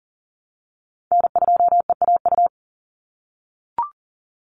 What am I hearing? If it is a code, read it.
Morse code, heard as a keyed tone, about 40 wpm, 700 Hz. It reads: N2EAU
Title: USA Call Signs 40wpm